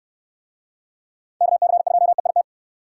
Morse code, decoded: KC2EIT

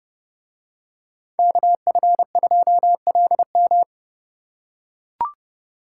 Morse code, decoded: KF2LM

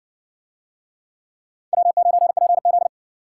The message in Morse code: W9PZ

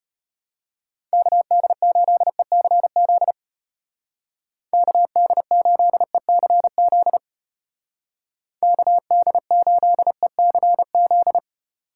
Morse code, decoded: KD8ECZ KD8ECZ KD8ECZ

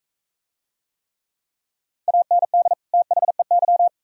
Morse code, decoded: AND THEY

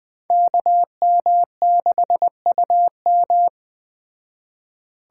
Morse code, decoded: KM6UM

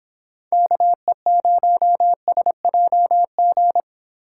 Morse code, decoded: KE0SJG

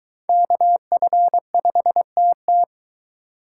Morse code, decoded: KF5TT

